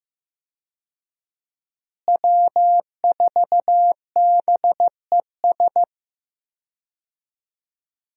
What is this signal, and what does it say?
Morse code, heard as a keyed tone, about 15 wpm, 700 Hz. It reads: W4BES
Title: USA Call Signs - 5 Characters - Repeated 2x - 15wpm